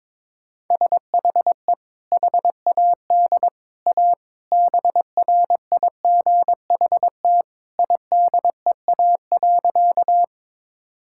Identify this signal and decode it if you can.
Morse code, heard as a keyed tone, about 22 wpm, 700 Hz. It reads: SHE HAD A BRIGHT IDEA.
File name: practice / 22wpm / Sentences with 2-Word Phrases - 22wpm